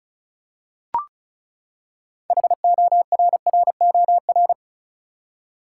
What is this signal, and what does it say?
Morse code, heard as a keyed tone, about 34 wpm, 700 Hz. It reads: HORROR
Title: Six Letter Words - 35wpm